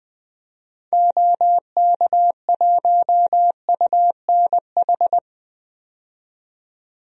Morse code, heard as a keyed tone, about 20 wpm, 700 Hz. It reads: OK1UNH
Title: DXCC-100 Call Signs - 6 Characters - 20wpm